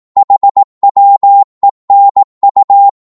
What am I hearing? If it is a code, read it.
Morse code, heard as a keyed tone, about 18 wpm, 800 Hz. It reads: HWENU